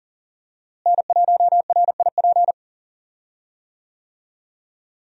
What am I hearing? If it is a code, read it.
Morse code, heard as a keyed tone, about 38 wpm, 700 Hz. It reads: N1RIP
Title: USA Call Signs - 5 Characters - Repeated 3x - 40wpm